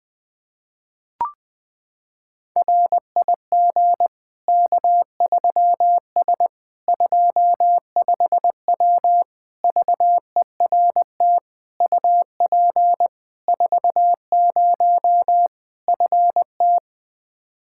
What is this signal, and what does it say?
Morse code, heard as a keyed tone, about 21 wpm, 700 Hz. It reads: RIG K3S 25W VERT UP 40 FT
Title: K1USN SST QSO Practice 20wpm